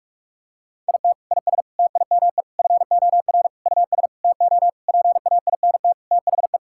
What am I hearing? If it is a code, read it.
Morse code, heard as a keyed tone, about 46 wpm, 700 Hz. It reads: IT IS TIME FOR US TO PAINT THE
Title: Sentences from Top 1000 Words - Review Part 6 - 45wpm